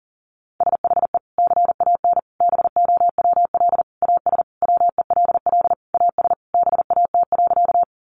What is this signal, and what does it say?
Morse code, heard as a keyed tone, about 40 wpm, 700 Hz. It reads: SHE CAN BOWL AS WELL AS BAT.
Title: Sentences from Top 1500 Words 40wpm